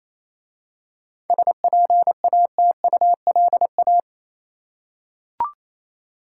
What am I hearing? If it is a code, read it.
Morse code, heard as a keyed tone, about 27 wpm, 700 Hz. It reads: SPATULA
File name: Easy Words - Speed-Racing with Hints - 22wpm